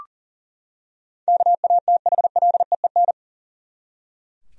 Morse code, heard as a keyed tone, about 41 wpm, 700 Hz. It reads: KATHLEEN
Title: USA Common Names - 40wpm